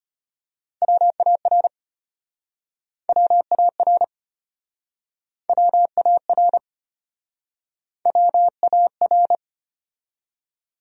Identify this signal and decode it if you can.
Morse code, heard as a keyed tone, about 35 wpm, 700 Hz. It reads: WAR WAR WAR WAR